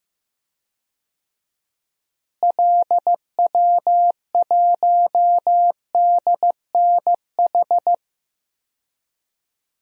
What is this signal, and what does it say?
Morse code, heard as a keyed tone, about 15 wpm, 700 Hz. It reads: LW1DNH
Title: DXCC-100 Call Signs - 6 Characters - Repeated 2x - 15wpm